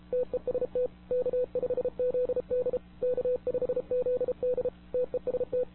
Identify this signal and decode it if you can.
Morse code, heard as a keyed tone, about 31 wpm, 500 Hz. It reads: TEST K5ZD K5ZD TEST
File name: wwcw08 / k5zd wwcw08 1700z